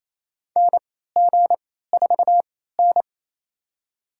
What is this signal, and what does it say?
Morse code, heard as a keyed tone, about 28 wpm, spaced as for 16 wpm, 700 Hz. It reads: NG4N
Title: USA Call Signs - 3x Character Spacing - 28wpm